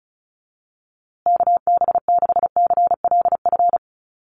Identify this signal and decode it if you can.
Morse code, heard as a keyed tone, about 34 wpm, 700 Hz. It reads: KB6CLF
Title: USA Call Signs Repeated 3x 35wpm